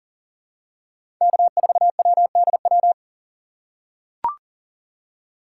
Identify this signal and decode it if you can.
Morse code, heard as a keyed tone, about 40 wpm, 700 Hz. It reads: K4WDW
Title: USA Call Signs - 4x Character Spacing - 40wpm